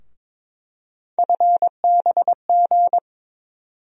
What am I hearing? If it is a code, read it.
Morse code, heard as a keyed tone, about 22 wpm, 700 Hz. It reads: FBG
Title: USA Call Sign Postfixes - 3 Letters 22wpm